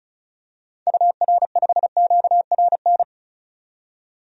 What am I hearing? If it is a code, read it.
Morse code, heard as a keyed tone, about 34 wpm, 700 Hz. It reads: UR5QRN
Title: DXCC-100 Call Signs - 6 Characters - 35wpm